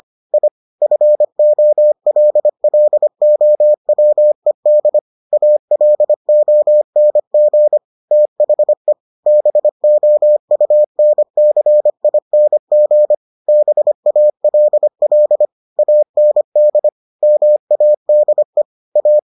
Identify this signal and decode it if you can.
Morse code, heard as a keyed tone, about 25 wpm, 600 Hz. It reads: I FOLLOWED ALONG THE BOUNCING BALL AND MADE A